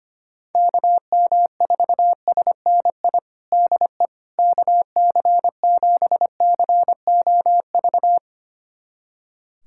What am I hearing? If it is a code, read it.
Morse code, heard as a keyed tone, about 25 wpm, 700 Hz. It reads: KM4SNI DE KC7COV
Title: USA Callsign DE Callsign 25wpm